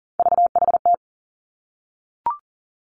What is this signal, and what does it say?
Morse code, heard as a keyed tone, about 40 wpm, 700 Hz. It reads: VHT
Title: Three Challenging Letter-Number Combinations 40wpm